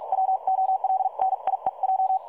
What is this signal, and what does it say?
Morse code, heard as a keyed tone, about 46 wpm, spaced as for 39 wpm, 760 Hz. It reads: WOLNIEJ
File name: dlazaawansowanych / nagraniazpasma / websdr recording 3565.0kHz